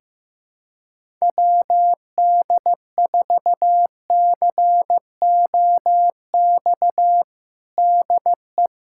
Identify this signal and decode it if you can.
Morse code, heard as a keyed tone, about 15 wpm, 700 Hz. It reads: WD4COX DE
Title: USA Callsign DE Callsign 15wpm